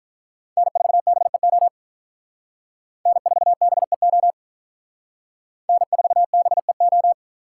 Morse code, heard as a keyed tone, about 45 wpm, 700 Hz. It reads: N4BEO N4BEO N4BEO